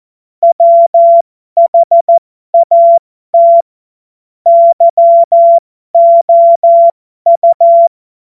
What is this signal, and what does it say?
Morse code, heard as a keyed tone, about 14 wpm, spaced as for 12 wpm, 680 Hz. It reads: WHAT YOU
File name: Morsekurs / Witze